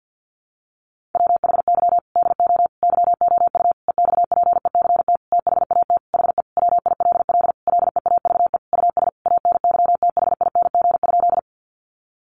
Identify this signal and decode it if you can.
Morse code, heard as a keyed tone, about 48 wpm, 700 Hz. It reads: WHY DO YOU EXPECT THAT HE WILL LEAVE US ANYTHING?
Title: Sentences from Top 1100 Words 50wpm